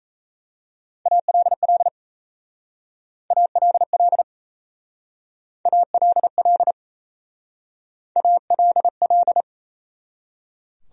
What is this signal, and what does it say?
Morse code, heard as a keyed tone, about 36 wpm, 700 Hz. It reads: ALL ALL ALL ALL